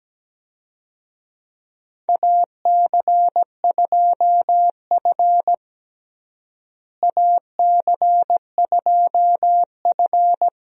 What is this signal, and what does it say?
Morse code, heard as a keyed tone, about 17 wpm, 700 Hz. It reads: AC2F AC2F
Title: USA Call Signs - 4 Characters - Repeated 3x - 17wpm